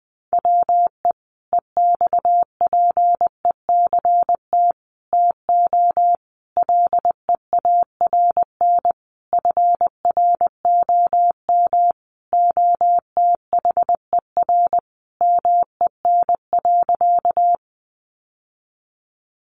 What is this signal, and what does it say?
Morse code, heard as a keyed tone, about 20 wpm, 700 Hz. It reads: WE EXPECT TO LEARN FROM OTHER MEN.
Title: Sentences from Top 1100 Words 20wpm